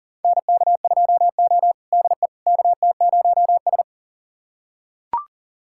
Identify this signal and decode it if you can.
Morse code, heard as a keyed tone, about 38 wpm, 700 Hz. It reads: NK2O DE KT0S
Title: USA Callsign DE Callsign - 4 Characters - 40wpm